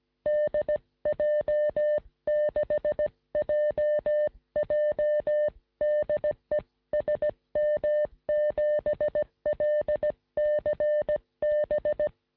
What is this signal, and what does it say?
Morse code, heard as a keyed tone, about 17 wpm, 590 Hz. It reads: DJ6JJ DE SM7LCB
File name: sm7lcb / 1647z dj6jj 3cm rs